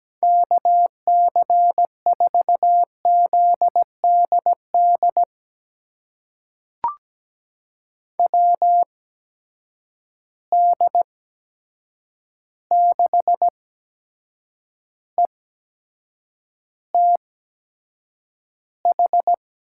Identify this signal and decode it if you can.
Morse code, heard as a keyed tone, about 17 wpm, spaced as for 7 wpm, 700 Hz. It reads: KC4ZDD WD6ETH
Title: USA Call Signs - 6 Characters - 8x Character Spacing - 17wpm